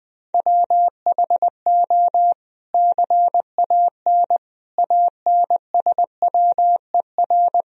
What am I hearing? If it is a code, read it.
Morse code, heard as a keyed tone, about 19 wpm, 700 Hz. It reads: WHO CAN ANSWER
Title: Sentences from Top 300 Words and Abbreviations 20wpm